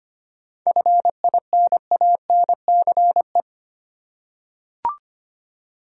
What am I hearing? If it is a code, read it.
Morse code, heard as a keyed tone, about 25 wpm, 700 Hz. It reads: FINANCE
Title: Seven Letter Words - 25wpm